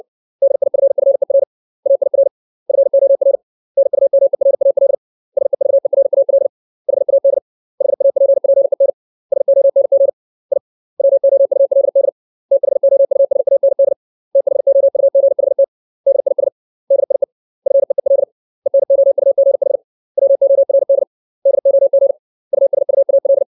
Morse code, heard as a keyed tone, about 60 wpm, 550 Hz. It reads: DIFFER AIR FOR NUMERAL SEVERAL HAD HAPPEN SONG I WOULD THOUSAND THOUGHT DIS DIE FEEL ENOUGH WORD DOG USUAL
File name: 632 words / 632 Words 60WPM